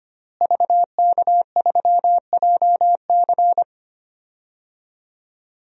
Morse code, heard as a keyed tone, about 25 wpm, 700 Hz. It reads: VK3JC
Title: DXCC-100 Call Signs - 5 Characters - Repeated 2x - 25wpm